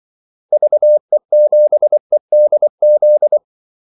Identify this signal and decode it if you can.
Morse code, heard as a keyed tone, about 24 wpm, 600 Hz. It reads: VE7EDZ
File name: Callsigns-24wpm